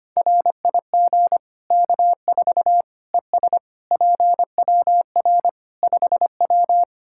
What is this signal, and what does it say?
Morse code, heard as a keyed tone, about 25 wpm, 700 Hz. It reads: RIG K4 ES PWR 5W
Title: K1USN SST QSO Practice 25wpm